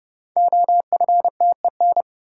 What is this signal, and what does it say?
Morse code, heard as a keyed tone, about 30 wpm, 700 Hz. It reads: OFTEN
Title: Sentences from Top 300 Words - New Word Drill - 30wpm